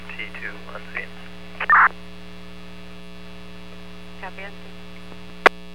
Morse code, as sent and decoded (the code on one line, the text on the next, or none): none